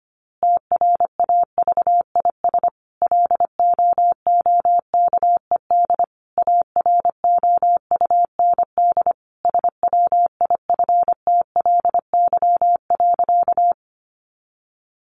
Morse code, heard as a keyed tone, about 25 wpm, 700 Hz. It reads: TRAVIS LOOKED AROUND SWIFTLY.
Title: A Sound Of Thunder 25wpm